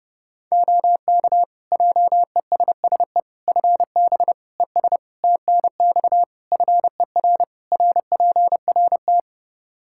Morse code, heard as a keyed tone, about 30 wpm, 700 Hz. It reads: OK JESSE FB ES TNX FER RPRT
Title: Practice QSO - Exchange 2 - Info - 30wpm